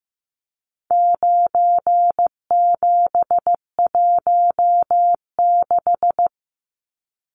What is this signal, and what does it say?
Morse code, heard as a keyed tone, about 15 wpm, 700 Hz. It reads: 9716
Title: Numbers - 4-Digit 15wpm